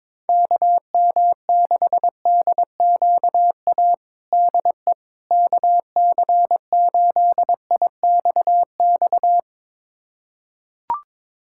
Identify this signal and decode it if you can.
Morse code, heard as a keyed tone, about 22 wpm, 700 Hz. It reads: KM6DQA DE KC8IXX
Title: USA Callsign DE Callsign 22wpm